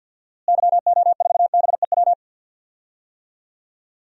Morse code, heard as a keyed tone, about 46 wpm, 700 Hz. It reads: YO4BEW
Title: DXCC-100 Call Signs - 6 Characters - Repeated 3x - 50wpm